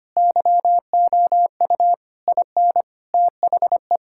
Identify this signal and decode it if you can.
Morse code, heard as a keyed tone, about 25 wpm, 700 Hz. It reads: YOU IN THE
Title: Sentences from Top 1000 Words - Review Part 6 - 25wpm